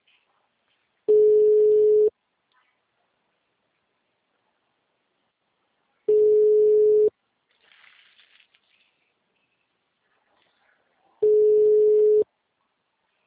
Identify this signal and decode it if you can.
Ringback tone (Europe, 425 Hz).
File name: out- -2019- - - .81048